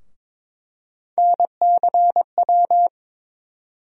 Morse code, heard as a keyed tone, about 22 wpm, 700 Hz. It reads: NCW
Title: USA Call Sign Postfixes - 3 Letters 22wpm